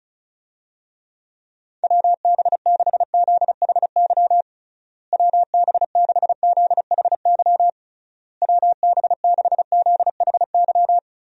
Morse code, heard as a keyed tone, about 34 wpm, 700 Hz. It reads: WB6ZHY WB6ZHY WB6ZHY
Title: USA Call Signs - 6 Characters - Repeated 3x - 35wpm